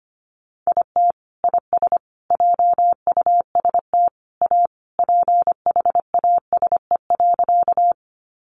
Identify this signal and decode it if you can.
Morse code, heard as a keyed tone, about 25 wpm, 700 Hz. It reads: IT IS JUST A PHASE.
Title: Sentences from Top 1200 Words 25wpm